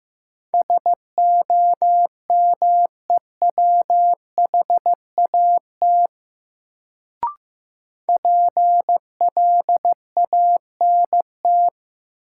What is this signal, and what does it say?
Morse code, heard as a keyed tone, about 15 wpm, 700 Hz. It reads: SOMEWHAT PLANT
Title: Sentences from Top 1000 Words - Review Part 1 - New Word Drill - 15wpm